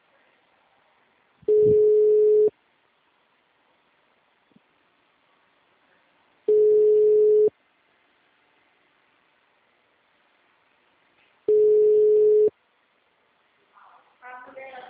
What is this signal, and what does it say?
Ringback tone (Europe, 425 Hz), with speech.
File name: out- -2014- - - .72892